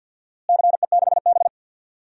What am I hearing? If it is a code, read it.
Morse code, heard as a keyed tone, about 52 wpm, 700 Hz. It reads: CE6B
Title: DXCC-100 Call Signs - 4 Characters - Repeated 2x - 50wpm